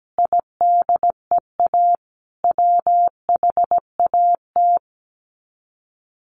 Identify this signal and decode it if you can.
Morse code, heard as a keyed tone, about 17 wpm, 700 Hz. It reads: IDEA WHAT
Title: Sets of 5 Words Encore 17wpm